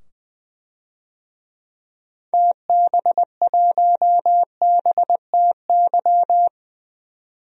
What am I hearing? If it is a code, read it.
Morse code, heard as a keyed tone, about 20 wpm, 700 Hz. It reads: TB1BTY
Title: DXCC-100 Call Signs - 6 Characters - 20wpm